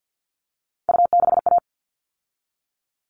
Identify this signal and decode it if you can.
Morse code, heard as a keyed tone, about 50 wpm, 700 Hz. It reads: U6A
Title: All Cut Numbers - 3-Digit 50wpm